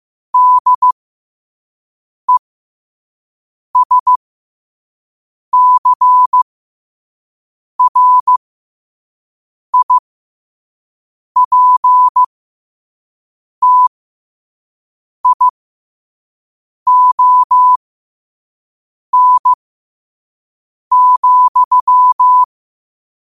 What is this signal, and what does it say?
Morse code, heard as a keyed tone, about 15 wpm, spaced as for 5 wpm, 1000 Hz. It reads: DESCRIPTION,
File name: qotdfiles / 5wpm